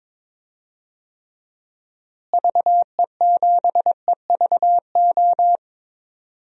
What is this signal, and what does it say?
Morse code, heard as a keyed tone, about 21 wpm, 700 Hz. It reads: VE7EVO
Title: DXCC-100 Call Signs - 6 Characters - Repeated 2x - 22wpm